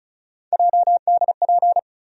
Morse code, heard as a keyed tone, about 36 wpm, 700 Hz. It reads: JDP
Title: USA Call Sign Postfixes - 3 Letters 35wpm